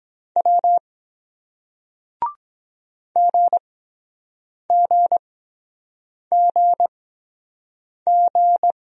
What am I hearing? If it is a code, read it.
Morse code, heard as a keyed tone, about 23 wpm, spaced as for 7 wpm, 700 Hz. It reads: W GGGG